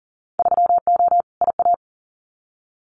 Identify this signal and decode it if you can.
Morse code, heard as a keyed tone, about 38 wpm, 700 Hz. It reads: 3O IA
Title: Winter Field Day Exchanges 40wpm